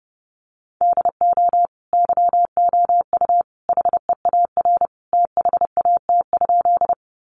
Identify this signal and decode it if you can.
Morse code, heard as a keyed tone, about 30 wpm, 700 Hz. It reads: DO YOU HEAR THAT?
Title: Sentences from Top 300 Words 30wpm